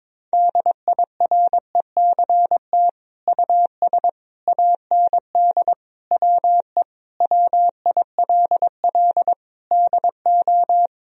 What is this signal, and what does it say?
Morse code, heard as a keyed tone, about 22 wpm, 700 Hz. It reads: DIRECT US AND WE WILL DO
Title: Sentences from Top 1000 Words - Review Part 2 - 22wpm